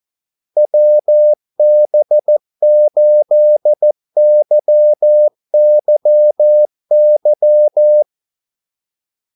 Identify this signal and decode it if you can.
Morse code, heard as a keyed tone, about 14 wpm, 600 Hz. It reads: WB8YYY
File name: Callsigns-14wpm